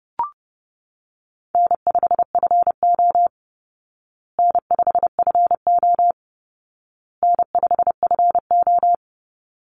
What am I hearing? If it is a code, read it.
Morse code, heard as a keyed tone, about 30 wpm, 700 Hz. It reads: N5FO N5FO N5FO
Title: Easy Call Signs Repeated 3x - 30wpm